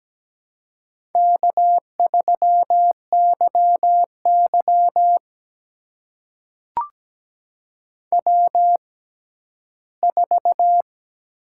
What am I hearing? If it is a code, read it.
Morse code, heard as a keyed tone, about 17 wpm, spaced as for 12 wpm, 700 Hz. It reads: K3YY W4
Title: USA Call Signs - 4 Characters - 6x Character Spacing - 17wpm